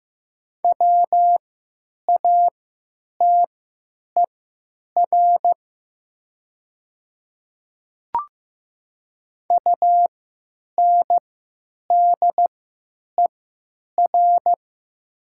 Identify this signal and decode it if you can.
Morse code, heard as a keyed tone, about 15 wpm, spaced as for 9 wpm, 700 Hz. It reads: WATER UNDER
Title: 3x Character Spacing - Top 300 Words - 15wpm